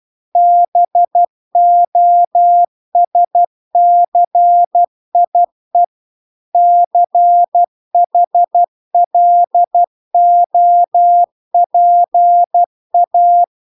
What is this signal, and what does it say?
Morse code, heard as a keyed tone, about 12 wpm, 700 Hz. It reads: BOSCIE CHLOPA
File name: recordings / 12wpm0000